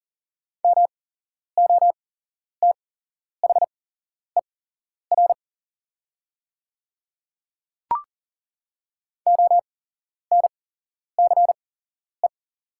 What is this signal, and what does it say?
Morse code, heard as a keyed tone, about 38 wpm, spaced as for 11 wpm, 700 Hz. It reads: MOTHER ONCE
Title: 8x Character Spacing - Top 500 Words - 40wpm